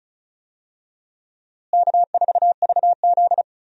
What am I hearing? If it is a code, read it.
Morse code, heard as a keyed tone, about 36 wpm, 700 Hz. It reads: K4VZ
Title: USA Call Signs - 4 Characters - Repeated 3x - 35wpm